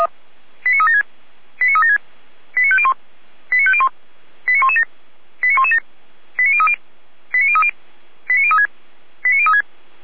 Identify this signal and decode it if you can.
Five-tone selective call (ZVEI), heard as quick runs of five tones, tones of about 70 ms. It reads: 89376 89376 89501 89501 89117 89117 89030 89030 89036 89036